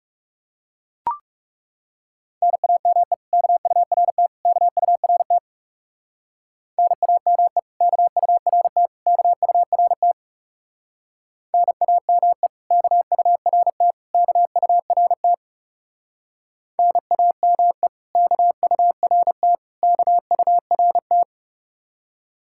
Morse code, heard as a keyed tone, about 34 wpm, 700 Hz. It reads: NAME KURT KURT NAME KURT KURT NAME KURT KURT NAME KURT KURT
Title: Speed-Racing Common QSO Phrases Encore - 30wpm